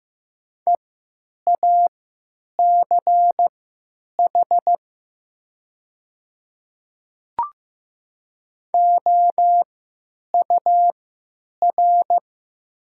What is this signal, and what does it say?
Morse code, heard as a keyed tone, about 15 wpm, spaced as for 9 wpm, 700 Hz. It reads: EACH OUR
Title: 3x Character Spacing - Top 200 Words - 15wpm